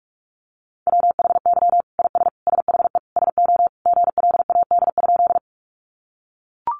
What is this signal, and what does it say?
Morse code, heard as a keyed tone, about 44 wpm, 700 Hz. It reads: WHY IS SHE SO GLAD?
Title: Sentences from Top 800 Words 45wpm